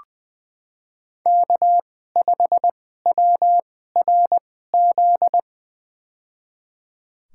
Morse code, heard as a keyed tone, about 21 wpm, spaced as for 15 wpm, 700 Hz. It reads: K5WRZ
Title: USA Call Signs - 5 Characters - 2x Character Spacing - 20wpm